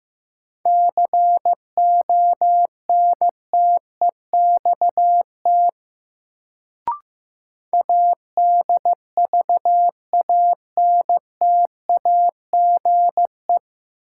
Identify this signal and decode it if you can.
Morse code, heard as a keyed tone, about 15 wpm, 700 Hz. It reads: CONTEXT ADVANTAGE
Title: Sets of 4 Words - New Word Drill - 15wpm